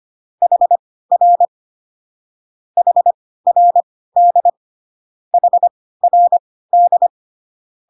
Morse code, heard as a keyed tone, about 25 wpm, spaced as for 16 wpm, 700 Hz. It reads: HR HRD HRD